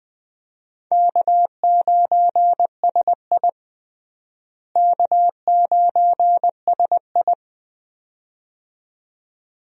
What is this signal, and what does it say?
Morse code, heard as a keyed tone, about 21 wpm, 700 Hz. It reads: K9SI K9SI
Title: USA Call Signs - 4 Characters - Repeated 2x - 20wpm